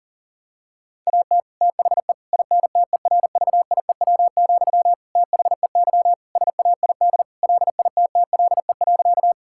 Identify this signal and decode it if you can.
Morse code, heard as a keyed tone, about 40 wpm, 700 Hz. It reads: AT THE INTERVIEW, THEY SAID LITTLE.
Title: Sentences from Top 1600 Words 40wpm